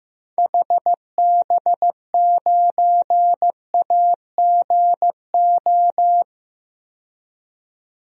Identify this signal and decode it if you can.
Morse code, heard as a keyed tone, about 15 wpm, 700 Hz. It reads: HB9AGO
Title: DXCC-100 Call Signs - 6 Characters - 15wpm